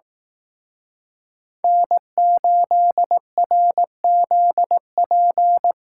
Morse code, heard as a keyed tone, about 18 wpm, 700 Hz. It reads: N8RZP
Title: Speed Racing - USA 1x3 Call Signs 15wpm